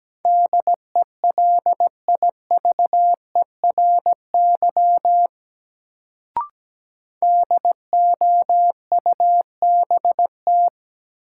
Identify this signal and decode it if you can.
Morse code, heard as a keyed tone, about 17 wpm, 700 Hz. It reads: DELIVERY DOUBT
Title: Sentences from Top 1500 Words - New Word Drill - 17wpm